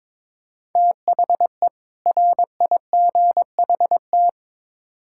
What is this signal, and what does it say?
Morse code, heard as a keyed tone, about 22 wpm, 700 Hz. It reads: THE RIGHT
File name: Build A Phrase - 2-Word Phrases - 22wpm